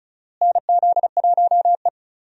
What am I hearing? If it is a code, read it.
Morse code, heard as a keyed tone, about 36 wpm, 700 Hz. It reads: NZ1E
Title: USA Call Signs - 4 Characters - Repeated 2x - 35wpm